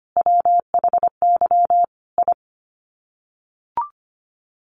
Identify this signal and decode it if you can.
Morse code, heard as a keyed tone, about 25 wpm, 700 Hz. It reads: WHY I
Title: Sets of 5 Words Encore 25wpm